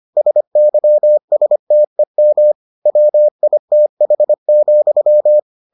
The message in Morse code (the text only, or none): SYSTEM WITH,